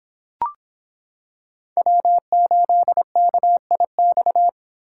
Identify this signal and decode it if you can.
Morse code, heard as a keyed tone, about 25 wpm, 700 Hz. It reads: W8KIX